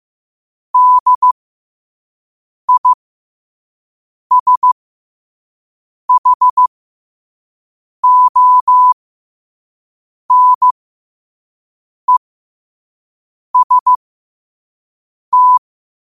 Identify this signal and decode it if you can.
Morse code, heard as a keyed tone, about 15 wpm, spaced as for 5 wpm, 1000 Hz. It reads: DISHONEST
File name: qotdfiles / 5wpm